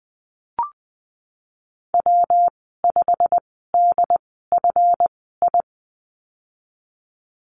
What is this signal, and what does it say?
Morse code, heard as a keyed tone, about 20 wpm, spaced as for 15 wpm, 700 Hz. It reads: W5DFI
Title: 154 Lesson 25g - Letter p - Call Signs 20wpm